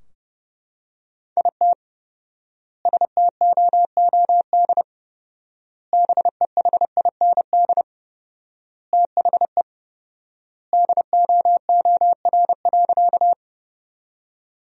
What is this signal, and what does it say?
Morse code, heard as a keyed tone, about 30 wpm, 700 Hz. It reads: IT STOOD BEHIND THE DOOR.
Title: 4x Word-Spacing - Sentences from Top 500 Words - 30wpm